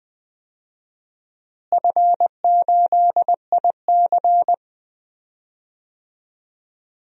Morse code, heard as a keyed tone, about 20 wpm, 700 Hz. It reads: F8IC
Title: DXCC-100 Call Signs - 4 Characters - Repeated 2x - 20wpm